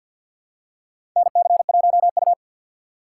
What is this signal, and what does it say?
Morse code, heard as a keyed tone, about 52 wpm, 700 Hz. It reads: NC1U
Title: USA Call Signs - 4 Characters - 6x Character Spacing - 50wpm